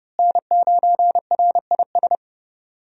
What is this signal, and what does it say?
Morse code, heard as a keyed tone, about 32 wpm, 700 Hz. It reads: N9RIS
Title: USA Call Signs - 5 Characters - Repeated 2x - 30wpm